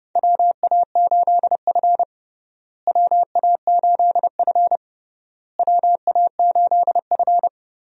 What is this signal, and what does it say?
Morse code, heard as a keyed tone, about 30 wpm, 700 Hz. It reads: WA8F WA8F WA8F
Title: USA Call Signs - 4 Characters - Repeated 3x - 30wpm